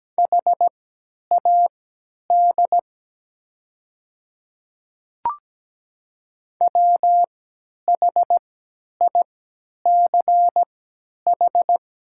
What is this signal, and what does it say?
Morse code, heard as a keyed tone, about 17 wpm, spaced as for 10 wpm, 700 Hz. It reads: HAD WHICH
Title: 3x Character Spacing - Top 100 Words - 17wpm